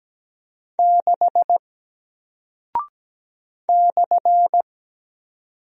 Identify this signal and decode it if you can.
Morse code, heard as a keyed tone, about 17 wpm, 700 Hz. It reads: 6 /
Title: Single Character (Letter:Number:Punctuation:Prosign) 17wpm